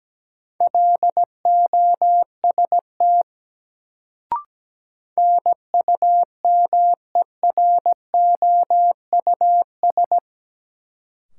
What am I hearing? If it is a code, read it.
Morse code, heard as a keyed tone, about 17 wpm, 700 Hz. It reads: LOST NUMEROUS
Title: Sentences from Top 1400 Words - New Word Drill - 17wpm